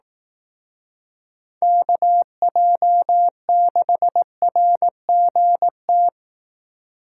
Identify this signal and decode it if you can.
Morse code, heard as a keyed tone, about 18 wpm, 700 Hz. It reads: KJ6RGT
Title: Speed Racing - USA 2x3 Call Signs 15wpm